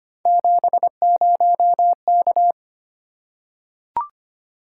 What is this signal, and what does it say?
Morse code, heard as a keyed tone, about 25 wpm, 700 Hz. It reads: 70K
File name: Three Letter-Number Combinations - 25wpm